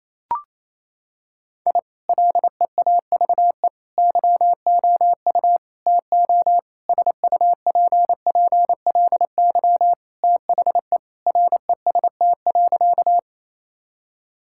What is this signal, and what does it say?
Morse code, heard as a keyed tone, about 28 wpm, 700 Hz. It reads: I LEAVE YOU TO SUPPLY THE REST.
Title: Sentences from Top 1000 Words - Review Part 5 - 28wpm